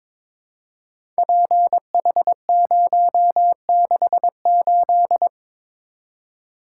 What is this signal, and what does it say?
Morse code, heard as a keyed tone, about 22 wpm, 700 Hz. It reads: PH068
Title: Code Groups - 5 Letter:Numbers - 22wpm